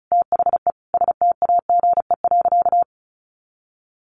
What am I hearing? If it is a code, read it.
Morse code, heard as a keyed tone, about 34 wpm, 700 Hz. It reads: THE STAGE.
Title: Sentences from Top 1400 Words 35wpm